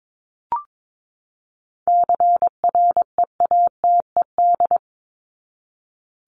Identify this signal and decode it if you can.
Morse code, heard as a keyed tone, about 22 wpm, 700 Hz. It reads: CREATED
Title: ED Suffix - Top 100 Words with Base Words 22wpm